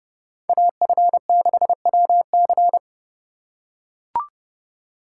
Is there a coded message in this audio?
Yes: Morse code from 0.5 to 2.8 s, heard as a keyed tone, about 28 wpm, 700 Hz. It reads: AF6WC